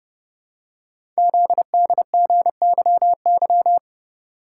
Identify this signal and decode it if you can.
Morse code, heard as a keyed tone, about 30 wpm, 700 Hz. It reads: ZDGYY
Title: Code Groups - 5 Letters - 30wpm